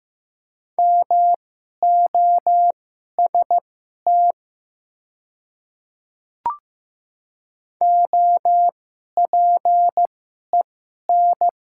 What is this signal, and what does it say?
Morse code, heard as a keyed tone, about 15 wpm, spaced as for 11 wpm, 700 Hz. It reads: MOST OPEN
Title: 2x Character Spacing - Top 300 Words - 15wpm